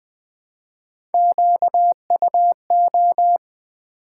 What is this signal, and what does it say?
Morse code, heard as a keyed tone, about 20 wpm, 700 Hz. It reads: QUO
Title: Build A Word - 1 Letter At A Time - 5-Letter Words - 20wpm